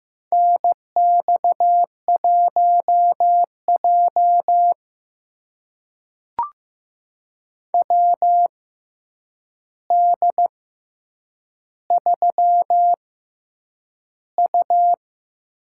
Morse code, heard as a keyed tone, about 15 wpm, spaced as for 8 wpm, 700 Hz. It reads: NX1J WD3U
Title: USA Call Signs - 4 Characters - 6x Character Spacing - 15wpm